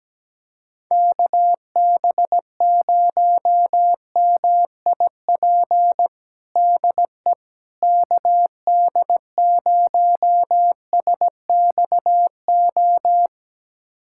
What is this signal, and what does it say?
Morse code, heard as a keyed tone, about 17 wpm, 700 Hz. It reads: KB0MIP DE KD0SXO
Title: USA Callsign DE Callsign 17wpm